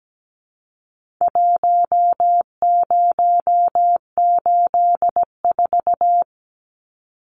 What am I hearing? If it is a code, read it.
Morse code, heard as a keyed tone, about 17 wpm, 700 Hz. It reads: 1084
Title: Numbers - Contest Style - 4-Digit 17wpm